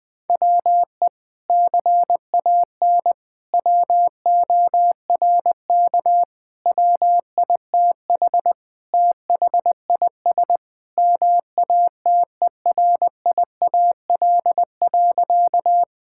Morse code, heard as a keyed tone, about 21 wpm, 700 Hz. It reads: WE CAN WORK WITH THIS MATERIAL.
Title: Sentences from Top 1000 Words - Review Part 5 - 20wpm